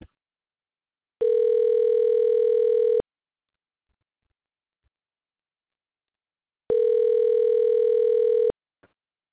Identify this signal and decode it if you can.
Ringback tone (North America), 440 + 480 Hz.
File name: recordings / REe7f5212c8f61b1003cab14fa824e849d